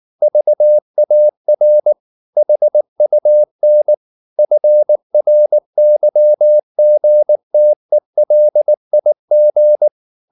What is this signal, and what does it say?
Morse code, heard as a keyed tone, about 19 wpm, 600 Hz. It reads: VAR HUN FRYGTELIG
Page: Ferie 19 wpm